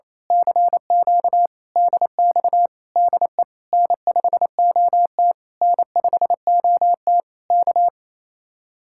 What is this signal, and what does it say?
Morse code, heard as a keyed tone, about 26 wpm, 700 Hz. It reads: CQ DX DE N5OT N5OT K
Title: Speed-Racing Common QSO Phrases - 28wpm